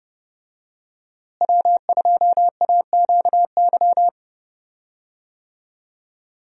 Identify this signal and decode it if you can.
Morse code, heard as a keyed tone, about 30 wpm, 700 Hz. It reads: W2AQY
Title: USA Call Signs - 5 Characters - Repeated 3x - 30wpm